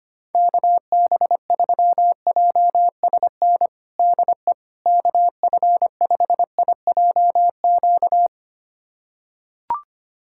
Morse code, heard as a keyed tone, about 25 wpm, 700 Hz. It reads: KB3JSN DE KF5IJQ
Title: USA Callsign DE Callsign - 6 Characters - 25wpm